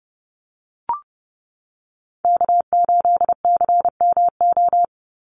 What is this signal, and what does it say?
Morse code, heard as a keyed tone, about 30 wpm, 700 Hz. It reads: K8CMO
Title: Call Signs - 30wpm